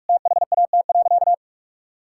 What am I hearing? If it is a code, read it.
Morse code, heard as a keyed tone, about 46 wpm, 700 Hz. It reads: THAT.
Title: Sentences from Top 1600 Words 45wpm